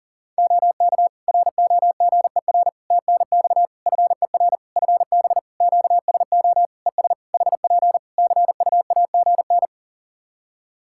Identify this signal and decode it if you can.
Morse code, heard as a keyed tone, about 40 wpm, 700 Hz. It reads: OK ROGER TNX FER FB QSO ES HP CUAGN
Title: Practice QSO - Exchange 4 - Ending - 40wpm